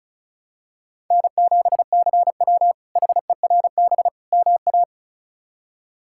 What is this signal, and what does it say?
Morse code, heard as a keyed tone, about 36 wpm, 700 Hz. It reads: N7CW HERB MA
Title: K1USN SST QSO Practice - 35wpm